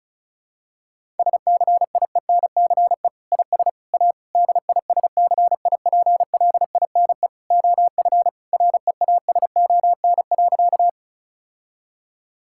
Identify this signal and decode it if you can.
Morse code, heard as a keyed tone, about 36 wpm, 700 Hz. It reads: SCIENCE IS A DISCIPLINE OF REASON.
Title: Sentences from Top 1000 Words - Review Part 6 - 35wpm